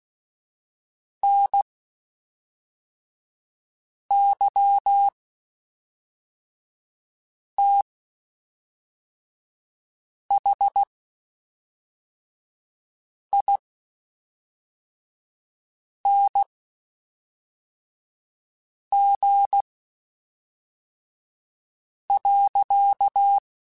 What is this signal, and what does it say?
Morse code, heard as a keyed tone, about 16 wpm, spaced as for 3 wpm, 800 Hz. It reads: NYTHING.
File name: Quotes Spaced / Quotes Spaced-mp3s / StandFor sp @5